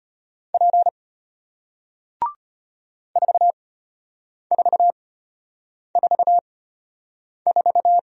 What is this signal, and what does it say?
Morse code, heard as a keyed tone, about 33 wpm, spaced as for 8 wpm, 700 Hz. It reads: P 4444